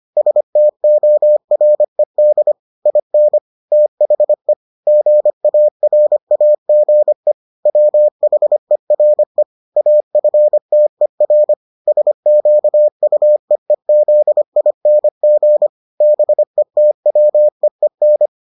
Morse code, heard as a keyed tone, about 25 wpm, 600 Hz. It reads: STORED IN THE GARAGE WHERE AFTER SQUEEZING BETWEEN